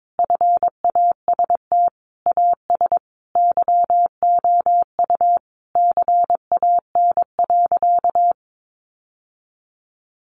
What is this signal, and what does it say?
Morse code, heard as a keyed tone, about 22 wpm, 700 Hz. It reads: FAST AS YOU CAN.
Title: Sentences from Top 600 Words 22wpm